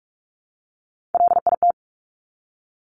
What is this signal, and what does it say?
Morse code, heard as a keyed tone, about 46 wpm, 700 Hz. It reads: LIT
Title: Top Three Letter Combinations in Words 45wpm